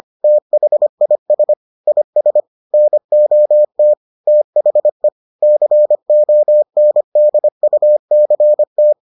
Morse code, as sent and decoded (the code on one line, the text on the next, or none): - .... .. ... / .. ... / -. --- - / - .... / -.-. --- -. -.. ..- -.-. -
THIS IS NOT THE CONDUCT